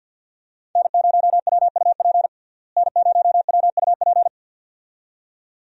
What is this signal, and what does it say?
Morse code, heard as a keyed tone, about 53 wpm, 700 Hz. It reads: N0WUP N0WUP
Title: USA Call Signs - 5 Characters - Repeated 2x - 50wpm